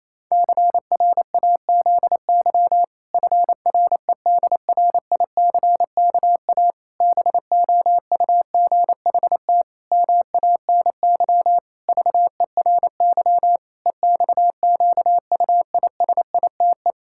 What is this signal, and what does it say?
Morse code, heard as a keyed tone, about 28 wpm, 700 Hz. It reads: CRAZY FREDRICKA BOUGHT MANY VERY EXQUISITE
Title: Pangram Challenge v2 - 28wpm